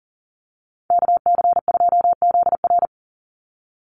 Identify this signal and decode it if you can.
Morse code, heard as a keyed tone, about 40 wpm, 700 Hz. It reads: KC2ZR
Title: USA Call Signs 40wpm